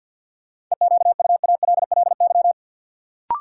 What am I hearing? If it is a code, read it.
Morse code, heard as a keyed tone, about 53 wpm, 700 Hz. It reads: EQUALLY
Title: Sentences from Top 1400 Words - New Word Drill - 50wpm